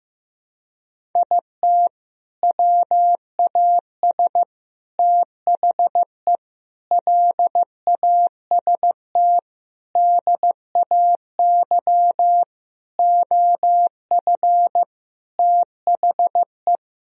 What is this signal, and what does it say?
Morse code, heard as a keyed tone, about 15 wpm, 700 Hz. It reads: IT WAS THE LAST DAY OF THE
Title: Sentences from Top 1000 Words - Review Part 1 - 15wpm